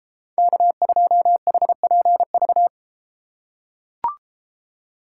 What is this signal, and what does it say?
Morse code, heard as a keyed tone, about 32 wpm, 700 Hz. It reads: K2HPV